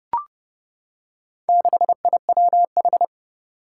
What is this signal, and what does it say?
Morse code, heard as a keyed tone, about 32 wpm, 700 Hz. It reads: 6IWH
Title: Four Challenging Letter-Number Combinations 30wpm